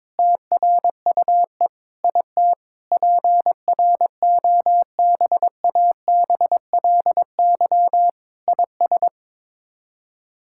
Morse code, heard as a keyed tone, about 22 wpm, 700 Hz. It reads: TRUE IT PROBABLY IS
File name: Sets of 9 Words - 22wpm